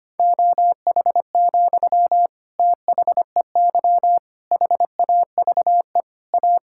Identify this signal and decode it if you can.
Morse code, heard as a keyed tone, about 25 wpm, 700 Hz. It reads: OH, THEY HAVE A
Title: Sentences from Top 1000 Words - Review Part 5 - 25wpm